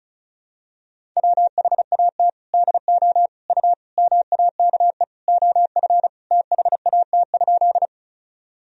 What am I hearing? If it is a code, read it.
Morse code, heard as a keyed tone, about 34 wpm, 700 Hz. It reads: WHAT DO U MAKE OF THAT?
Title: Sentences from Top 100 Words and Abbreviations 35wpm